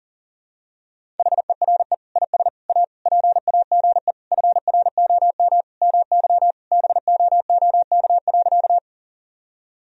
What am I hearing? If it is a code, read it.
Morse code, heard as a keyed tone, about 40 wpm, 700 Hz. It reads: HERE IS A PAGE FROM MY BOOK.
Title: Sentences from Top 1000 Words - Review Part 1 - 40wpm